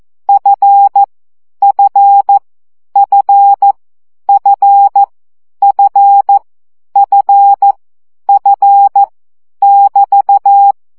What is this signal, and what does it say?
Morse code, heard as a keyed tone, about 14 wpm, 800 Hz. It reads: F F F F F F F <BT>
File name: CW gyakorlas f 12 baud